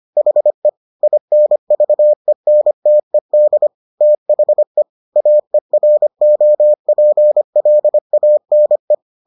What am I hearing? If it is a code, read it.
Morse code, heard as a keyed tone, about 25 wpm, 600 Hz. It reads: HE INVENTED THE AEROPLANE